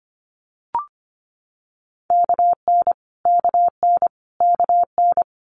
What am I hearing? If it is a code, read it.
Morse code, heard as a keyed tone, about 25 wpm, 700 Hz. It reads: KN KN KN
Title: Multi-letter Phonemes Repeated 3x 25wpm